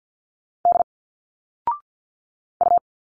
Morse code, heard as a keyed tone, about 52 wpm, 700 Hz. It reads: D U
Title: Single Letters 50wpm